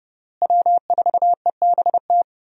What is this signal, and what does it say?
Morse code, heard as a keyed tone, about 30 wpm, 700 Hz. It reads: W4EBT
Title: USA Call Signs - 5 Characters - Repeated 3x - 30wpm